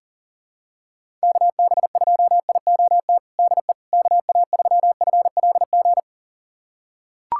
Morse code, heard as a keyed tone, about 37 wpm, 700 Hz. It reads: KB2IOT DE KA3FLG
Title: USA Callsign DE Callsign - 6 Characters - 40wpm